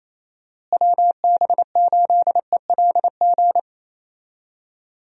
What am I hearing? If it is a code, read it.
Morse code, heard as a keyed tone, about 28 wpm, 700 Hz. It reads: WB8ELG
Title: USA Call Signs - 6 Characters - Repeated 2x - 28wpm